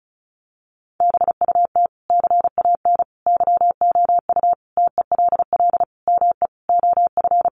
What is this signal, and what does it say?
Morse code, heard as a keyed tone, about 34 wpm, 700 Hz. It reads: BUT CAN YOU TELL ME OF
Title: Sentences from Top 1400 Words 35wpm